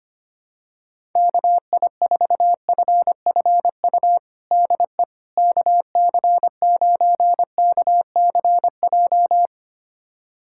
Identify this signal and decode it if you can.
Morse code, heard as a keyed tone, about 25 wpm, 700 Hz. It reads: KI4FFU DE KC9KCJ
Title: USA Callsign DE Callsign 25wpm